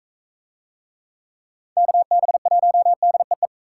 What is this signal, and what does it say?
Morse code, heard as a keyed tone, about 40 wpm, 700 Hz. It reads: KB1DEE